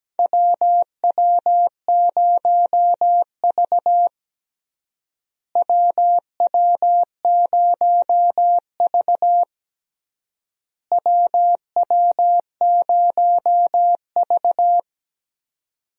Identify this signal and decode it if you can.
Morse code, heard as a keyed tone, about 17 wpm, 700 Hz. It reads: WW0V WW0V WW0V
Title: USA Call Signs - 4 Characters - Repeated 3x - 17wpm